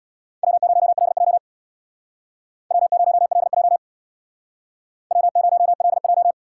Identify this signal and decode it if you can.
Morse code, heard as a keyed tone, about 61 wpm, 700 Hz. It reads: W0LJ W0LJ W0LJ
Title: Speed Racing - USA 1x2 Call Signs 50wpm